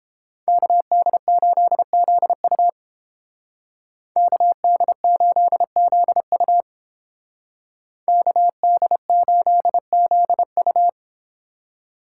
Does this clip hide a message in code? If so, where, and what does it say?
Yes: Morse code from 0.5 to 10.9 s, heard as a keyed tone, about 31 wpm, 700 Hz. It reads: KD8ZU KD8ZU KD8ZU